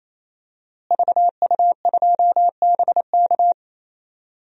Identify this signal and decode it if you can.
Morse code, heard as a keyed tone, about 27 wpm, 700 Hz. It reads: VU2BK
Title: DXCC-100 Call Signs - 5 Characters - Repeated 3x - 28wpm